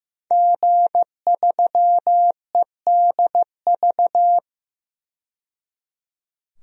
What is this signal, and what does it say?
Morse code, heard as a keyed tone, about 15 wpm, 700 Hz. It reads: G3EDV
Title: DXCC-100 Call Signs - 5 Characters - 15wpm